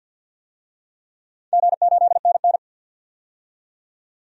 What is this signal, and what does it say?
Morse code, heard as a keyed tone, about 48 wpm, 700 Hz. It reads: G8NN
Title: DXCC-100 Call Signs - 4 Characters - 50wpm